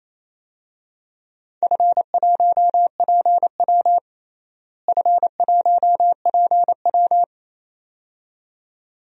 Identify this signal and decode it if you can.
Morse code, heard as a keyed tone, about 28 wpm, 700 Hz. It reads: F1PW F1PW
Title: DXCC-100 Call Signs - 4 Characters - Repeated 2x - 28wpm